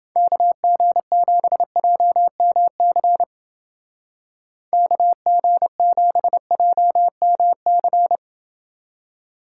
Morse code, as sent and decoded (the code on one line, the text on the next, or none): -.- --. --... .--- -- -.-. / -.- --. --... .--- -- -.-.
KG7JMC KG7JMC